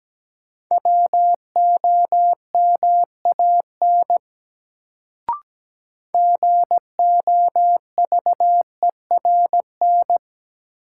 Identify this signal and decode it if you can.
Morse code, heard as a keyed tone, about 17 wpm, 700 Hz. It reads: WOMAN GOVERN
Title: Sentences from Top 1000 Words - Review Part 2 - New Word Drill - 17wpm